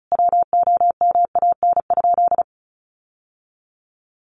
Morse code, heard as a keyed tone, about 34 wpm, 700 Hz. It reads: WOMAN?
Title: Sentences from Top 1100 Words 35wpm